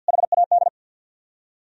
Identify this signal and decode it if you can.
Morse code, heard as a keyed tone, about 63 wpm, 700 Hz. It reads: HAD